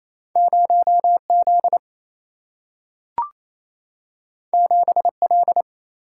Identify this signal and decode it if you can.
Morse code, heard as a keyed tone, about 28 wpm, 700 Hz. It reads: 0Z 7L
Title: Single-Letter Single-Number 28wpm